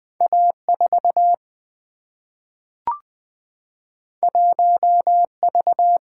Morse code, heard as a keyed tone, about 20 wpm, 700 Hz. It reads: A4 1V
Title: Two Letter-Number Combinations - 20wpm